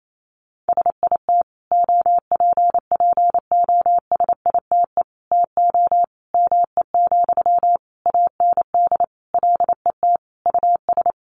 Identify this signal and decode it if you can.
Morse code, heard as a keyed tone, about 28 wpm, 700 Hz. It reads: SIT OPPOSITE TO ME, AND LET US
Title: Sentences from Top 1300 Words 28wpm